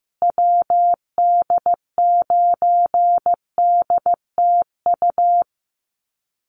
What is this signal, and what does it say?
Morse code, heard as a keyed tone, about 15 wpm, 700 Hz. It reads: WD9DTU
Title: USA Call Signs Repeated 3x 15wpm